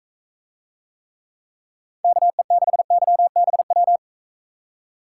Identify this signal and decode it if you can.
Morse code, heard as a keyed tone, about 41 wpm, 700 Hz. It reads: KE6YBW